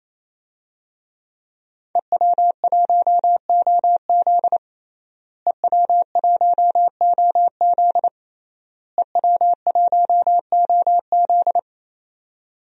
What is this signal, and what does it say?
Morse code, heard as a keyed tone, about 28 wpm, 700 Hz. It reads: EW1OZ EW1OZ EW1OZ
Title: DXCC-100 Call Signs - 5 Characters - Repeated 3x - 28wpm